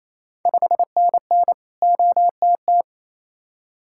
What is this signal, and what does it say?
Morse code, heard as a keyed tone, about 28 wpm, 700 Hz. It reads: 5NN OTT
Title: Ontario QSO Party Exchange 28wpm